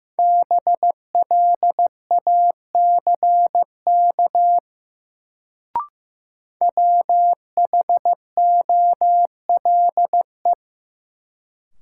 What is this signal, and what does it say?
Morse code, heard as a keyed tone, about 15 wpm, 700 Hz. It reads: BLACK WHOLE
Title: Sentences from Top 400 Words - New Word Drill - 15wpm